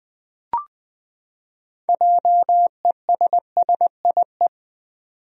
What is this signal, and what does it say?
Morse code, heard as a keyed tone, about 20 wpm, 700 Hz. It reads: JESSIE
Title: USA Common Names - 20wpm